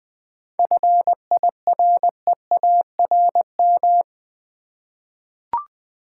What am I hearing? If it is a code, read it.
Morse code, heard as a keyed tone, about 20 wpm, 700 Hz. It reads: FIREARM
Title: 7 Letter Words - First 3 Letters Given - 20wpm